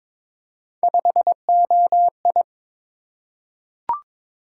Morse code, heard as a keyed tone, about 22 wpm, 700 Hz. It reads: 5OI
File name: practice / 22wpm / Three Letter-Number Combinations - 22wpm